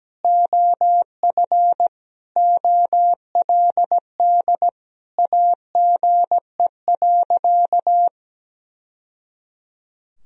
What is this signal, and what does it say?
Morse code, heard as a keyed tone, about 17 wpm, 700 Hz. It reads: OF OLD AGE.
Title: Sentences from Top 1000 Words - Review Part 3 - 17wpm